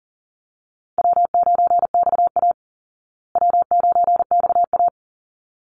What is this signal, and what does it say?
Morse code, heard as a keyed tone, about 37 wpm, 700 Hz. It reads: W9XA W9XA
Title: Easy Call Signs Repeated 2x - 40wpm